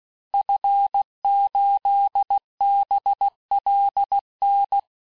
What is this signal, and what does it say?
Morse code, heard as a keyed tone, about 16 wpm, 800 Hz. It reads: F8BLN
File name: cours-du-lundi files / cours-1592--teste-FIN-de-QSO-vitesse-15mpm